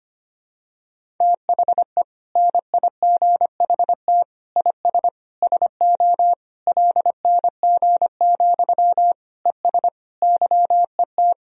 Morse code, heard as a keyed tone, about 25 wpm, 700 Hz. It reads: THE NIGHT IS SO LNG, ES YET
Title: Sentences from Top 500 Words and Abbreviations 25wpm